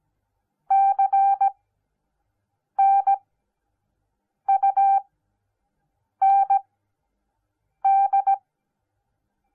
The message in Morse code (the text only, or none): CNUND